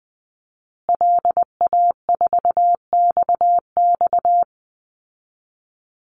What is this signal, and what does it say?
Morse code, heard as a keyed tone, about 20 wpm, 700 Hz. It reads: LA4XX
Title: Call Signs Repeated 2x - 20wpm